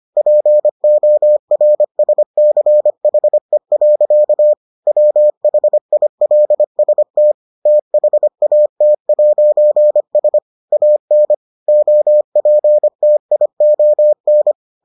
Morse code, heard as a keyed tone, about 25 wpm, 600 Hz. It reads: PORSCHE. WHILST THAT'S AN OPTION